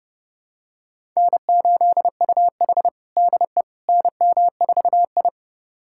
Morse code, heard as a keyed tone, about 32 wpm, 700 Hz. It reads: N8UH DE NM4I
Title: USA Callsign DE Callsign - 4 Characters - 30wpm